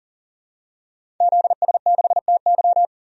Morse code, heard as a keyed tone, about 40 wpm, 700 Hz. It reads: ZS6TY
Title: DXCC-100 Call Signs - 5 Characters - 40wpm